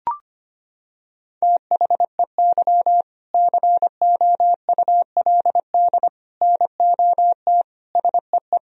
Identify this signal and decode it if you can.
Morse code, heard as a keyed tone, about 25 wpm, 700 Hz. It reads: THEY COULD NOT SEE
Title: Sentences from Top 1000 Words - Review Part 1 - 25wpm